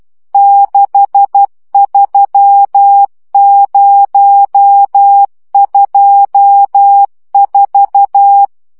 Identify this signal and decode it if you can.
Morse code, heard as a keyed tone, about 12 wpm, 800 Hz. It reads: 63024
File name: morze vizsga 2011-3 10